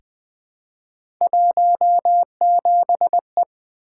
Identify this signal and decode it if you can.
Morse code, heard as a keyed tone, about 20 wpm, 700 Hz. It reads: 17E
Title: Three Letter-Number Combinations - 20wpm